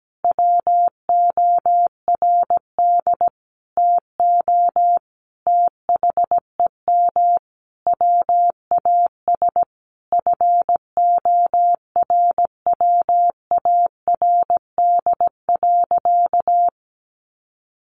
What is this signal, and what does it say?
Morse code, heard as a keyed tone, about 17 wpm, 700 Hz. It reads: WORD TO THEM WAS FORWARD.
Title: Sentences from Top 1500 Words 17wpm